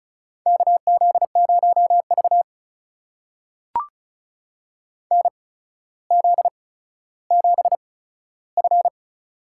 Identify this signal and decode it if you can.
Morse code, heard as a keyed tone, about 34 wpm, spaced as for 15 wpm, 700 Hz. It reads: KZ0V NZ7F
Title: USA Call Signs - 4 Characters - 8x Character Spacing - 35wpm